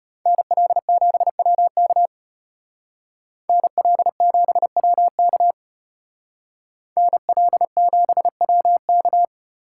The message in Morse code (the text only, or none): NL7WK NL7WK NL7WK